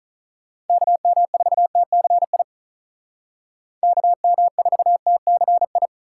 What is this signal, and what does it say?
Morse code, heard as a keyed tone, about 43 wpm, 700 Hz. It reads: KM4TCI KM4TCI
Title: Speed Racing - USA 2x3 Call Signs 35wpm